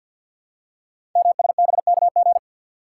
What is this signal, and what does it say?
Morse code, heard as a keyed tone, about 52 wpm, 700 Hz. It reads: MSBKG
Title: Code Groups - 5 Letters - 50wpm